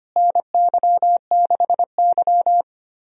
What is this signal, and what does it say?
Morse code, heard as a keyed tone, about 25 wpm, 700 Hz. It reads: NY6Y
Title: USA Call Signs - 4 Characters - Repeated 3x - 25wpm